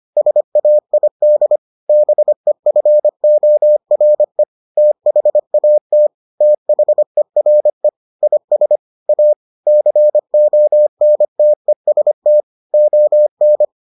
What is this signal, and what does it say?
Morse code, heard as a keyed tone, about 25 wpm, 600 Hz. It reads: SAID BEFORE THAT THERE IS A CONTEST ON